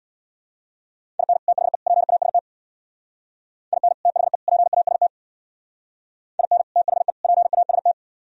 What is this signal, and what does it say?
Morse code, heard as a keyed tone, about 64 wpm, 700 Hz. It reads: IN THE PAST IN THE PAST IN THE PAST